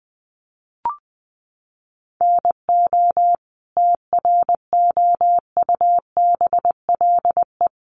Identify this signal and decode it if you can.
Morse code, heard as a keyed tone, about 20 wpm, 700 Hz. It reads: NO TROUBLE
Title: Sentences from Top 1200 Words 20wpm